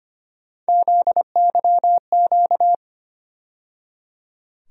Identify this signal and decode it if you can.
Morse code, heard as a keyed tone, about 25 wpm, 700 Hz. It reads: ZYQ
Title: USA Call Sign Postfixes - 3 Letters 25wpm